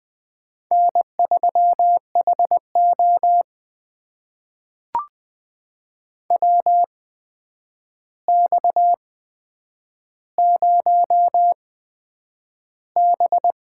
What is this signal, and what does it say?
Morse code, heard as a keyed tone, about 20 wpm, spaced as for 9 wpm, 700 Hz. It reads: N3HO WX0B
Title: USA Call Signs - 4 Characters - 8x Character Spacing - 20wpm